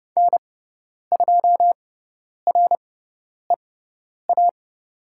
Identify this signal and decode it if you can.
Morse code, heard as a keyed tone, about 30 wpm, spaced as for 10 wpm, 690 Hz. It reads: N2REA
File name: Lesson 17g - Letter u - Call Signs 30wpm